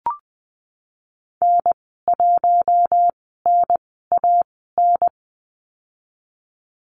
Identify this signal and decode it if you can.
Morse code, heard as a keyed tone, about 20 wpm, spaced as for 14 wpm, 700 Hz. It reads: N1NAN
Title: 042 Lesson 8g - Number 1 - Call Signs 20wpm